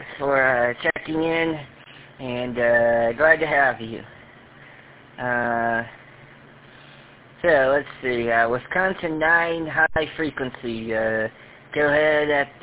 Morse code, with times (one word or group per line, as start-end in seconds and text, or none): none